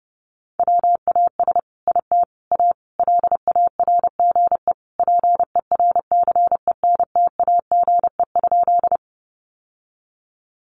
Morse code, heard as a keyed tone, about 30 wpm, 700 Hz. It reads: WAS IT A LARGE PERCENTAGE?
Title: Sentences from Top 1300 Words 30wpm